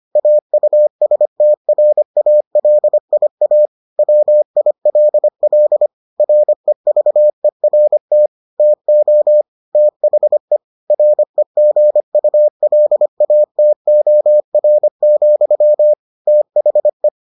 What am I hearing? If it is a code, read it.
Morse code, heard as a keyed tone, about 25 wpm, 600 Hz. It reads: AUSTRALIA WILL REVERT TO THE REGULATOR, THE